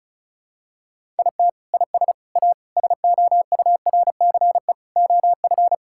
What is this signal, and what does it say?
Morse code, heard as a keyed tone, about 34 wpm, 700 Hz. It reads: IT IS A SOURCE OF
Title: Sentences from Top 1000 Words - Review Part 5 - 35wpm